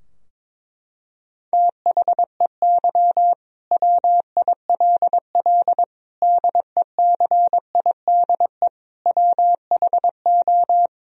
Morse code, heard as a keyed tone, about 22 wpm, 700 Hz. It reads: THEY WILL DECIDE WHO
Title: Sentences from Top 1000 Words - Review Part 3 - 22wpm